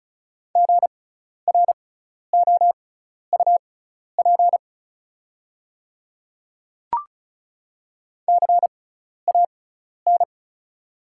Morse code, heard as a keyed tone, about 36 wpm, spaced as for 12 wpm, 690 Hz. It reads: GROUP CAN
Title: 6x Character Spacing - Top 300 Words - 35wpm